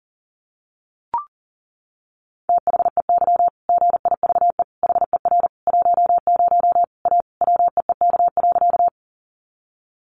Morse code, heard as a keyed tone, about 40 wpm, 700 Hz. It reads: THEY GIVE HER 10 A WEEK.
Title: Easy Phrases 40wpm